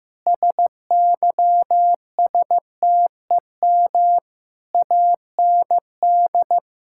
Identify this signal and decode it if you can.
Morse code, heard as a keyed tone, about 15 wpm, 700 Hz. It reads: SYSTEM AND
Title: Sentences from Top 1000 Words - Review Part 5 - 15wpm